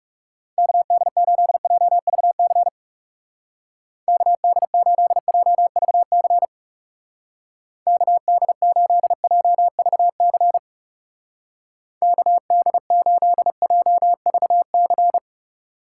Morse code, heard as a keyed tone, about 37 wpm, 700 Hz. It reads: KD8JVC KD8JVC KD8JVC KD8JVC
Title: Speed Racing - USA Call Signs 30wpm